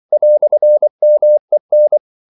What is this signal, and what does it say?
Morse code, heard as a keyed tone, about 24 wpm, 600 Hz. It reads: "MEN
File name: CW / Practice / EnglishQuotations0424